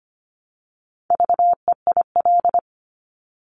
Morse code, heard as a keyed tone, about 25 wpm, 700 Hz. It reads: VEIL
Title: Multi-letter Phonemes Repeated 3x 25wpm